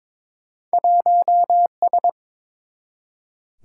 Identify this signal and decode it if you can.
Morse code, heard as a keyed tone, about 22 wpm, 700 Hz. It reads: 1S
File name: practice / 22wpm / Two Letter-Number Combinations - 22wpm